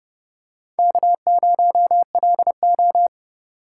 Morse code, heard as a keyed tone, about 30 wpm, 700 Hz. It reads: K0LO
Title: USA Call Signs - 4 Characters - 3x Character Spacing - 30wpm